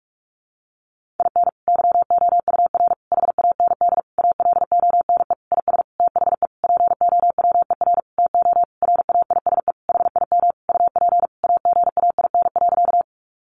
Morse code, heard as a keyed tone, about 44 wpm, 700 Hz. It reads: IN YOUR HAND ALONE IS THE POWER TO RAISE HIM UP AGAIN.
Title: Sentences from Top 1400 Words 45wpm